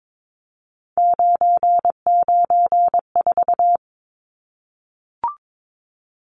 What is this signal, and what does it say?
Morse code, heard as a keyed tone, about 22 wpm, 700 Hz. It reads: 994
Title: Numbers - 3-Digit 22wpm